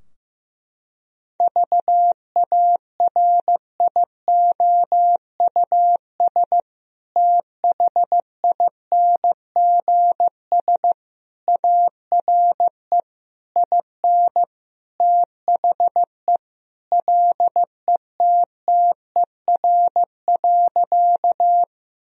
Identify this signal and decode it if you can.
Morse code, heard as a keyed tone, about 15 wpm, 700 Hz. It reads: VARIOUS THINGS ARE IN THE LETTER.
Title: Sentences from Top 1000 Words - Review Part 4 - 15wpm